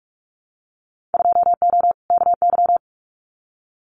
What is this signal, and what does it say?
Morse code, heard as a keyed tone, about 46 wpm, 700 Hz. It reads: 2O KY
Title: Winter Field Day Exchanges 45wpm